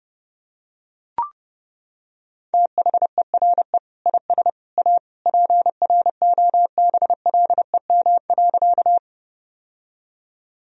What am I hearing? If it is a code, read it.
Morse code, heard as a keyed tone, about 30 wpm, 700 Hz. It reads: THERE IS A PROBLEM.
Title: Sentences from Top 1000 Words - Review Part 2 - 30wpm